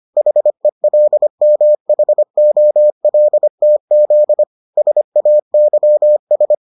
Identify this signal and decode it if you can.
Morse code, heard as a keyed tone, about 25 wpm, 600 Hz. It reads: HELMHOLTZ SAYS